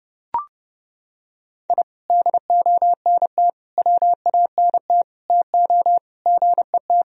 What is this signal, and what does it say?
Morse code, heard as a keyed tone, about 30 wpm, 700 Hz. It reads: I DONT WANT TO GET
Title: Sentences from Top 1000 Words - Review Part 4 - 30wpm